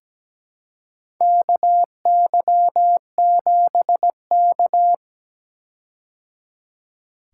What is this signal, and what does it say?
Morse code, heard as a keyed tone, about 17 wpm, 700 Hz. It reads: KY7K
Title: K1USN SST QSO Practice - 17wpm